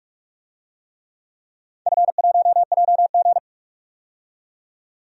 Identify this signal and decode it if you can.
Morse code, heard as a keyed tone, about 46 wpm, 700 Hz. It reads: F1JG
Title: DXCC-100 Call Signs - 4 Characters - 45wpm